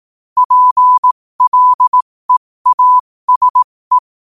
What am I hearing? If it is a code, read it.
Morse code, heard as a keyed tone, about 18 wpm, spaced as for 15 wpm, 1000 Hz. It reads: PLEASE
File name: qotdfiles / 15wpm